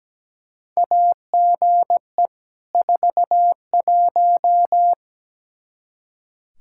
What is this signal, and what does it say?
Morse code, heard as a keyed tone, about 17 wpm, 700 Hz. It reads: AGE 41
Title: Practice QSO - Exchange 3 - More Info - 17wpm